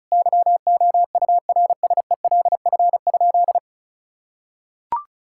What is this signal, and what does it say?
Morse code, heard as a keyed tone, about 36 wpm, 700 Hz. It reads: YOURSELF?
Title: Sentences from Top 1600 Words 35wpm